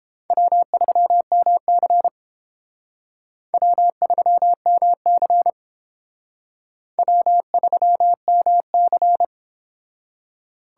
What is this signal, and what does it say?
Morse code, heard as a keyed tone, about 30 wpm, 700 Hz. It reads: W3MC W3MC W3MC